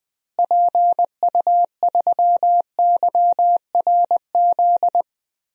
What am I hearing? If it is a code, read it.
Morse code, heard as a keyed tone, about 20 wpm, 700 Hz. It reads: PU3YRZ
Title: DXCC-100 Call Signs - 6 Characters - Repeated 3x - 20wpm